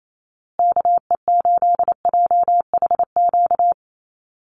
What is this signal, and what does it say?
Morse code, heard as a keyed tone, about 28 wpm, 700 Hz. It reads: KE8JHQ
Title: USA Call Signs Repeated 2x 28wpm